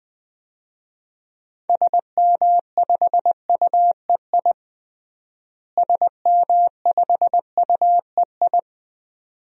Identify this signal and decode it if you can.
Morse code, heard as a keyed tone, about 20 wpm, 700 Hz. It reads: SM5UEI SM5UEI
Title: DXCC-100 Call Signs - 6 Characters - Repeated 2x - 20wpm